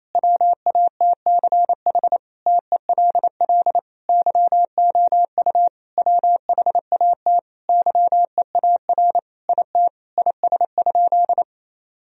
Morse code, heard as a keyed tone, about 28 wpm, 700 Hz. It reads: WATCH TELL YOU WHAT YEAR IT IS?
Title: Sentences from Top 1000 Words - Review Part 1 - 28wpm